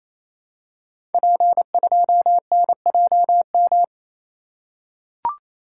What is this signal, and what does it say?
Morse code, heard as a keyed tone, about 28 wpm, 700 Hz. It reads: P2NJM
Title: Code Groups - 5 Letter:Numbers - 28wpm